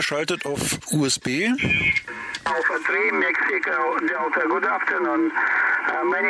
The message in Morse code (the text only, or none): none